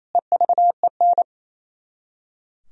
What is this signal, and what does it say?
Morse code, heard as a keyed tone, about 35 wpm, 700 Hz. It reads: EVEN
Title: Speed-Racing - Sets of 4 Words 28wpm